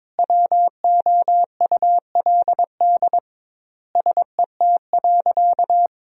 Morse code, heard as a keyed tone, about 22 wpm, 700 Hz. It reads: WOULD SET.
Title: 2x Word-Spacing - Sentences from Top 400 Words - 22wpm